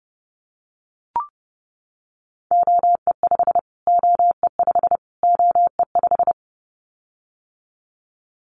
Morse code, heard as a keyed tone, about 30 wpm, 700 Hz. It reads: OE5 OE5 OE5
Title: Callsign Prefixes for Top 25 DXCC - Repeated 3x - 30wpm